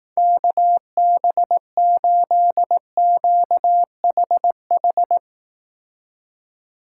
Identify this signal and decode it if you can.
Morse code, heard as a keyed tone, about 18 wpm, 700 Hz. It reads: KB8QHH
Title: Speed Racing - USA 2x3 Call Signs 15wpm